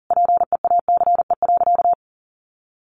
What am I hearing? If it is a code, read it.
Morse code, heard as a keyed tone, about 40 wpm, 700 Hz. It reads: PEACE.
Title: A Sound Of Thunder 40wpm